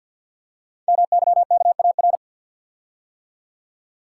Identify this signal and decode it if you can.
Morse code, heard as a keyed tone, about 50 wpm, 690 Hz. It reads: MYKAR
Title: Code Groups - 5 Letters - 50wpm